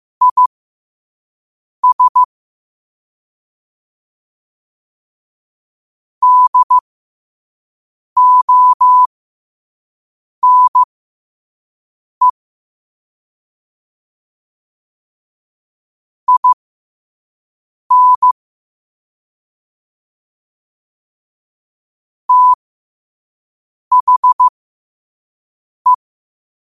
Morse code, heard as a keyed tone, about 15 wpm, spaced as for 5 wpm, 1000 Hz. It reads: IS DONE IN THE
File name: qotdfiles / 5wpm